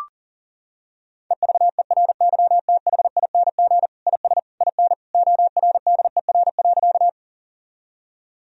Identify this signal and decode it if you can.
Morse code, heard as a keyed tone, about 40 wpm, 700 Hz. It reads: EVERYTHING IS IN ORDER.
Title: Sentences from Top 1000 Words - Review Part 2 - 40wpm